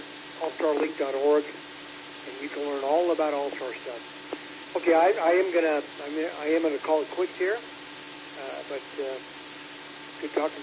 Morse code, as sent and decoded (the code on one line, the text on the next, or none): none